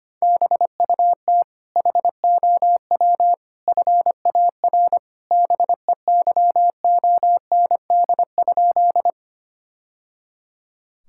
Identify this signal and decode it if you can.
Morse code, heard as a keyed tone, about 25 wpm, 700 Hz. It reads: BUT HOW FAR BEYOND?
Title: Sentences from Top 1000 Words - Review Part 6 - 25wpm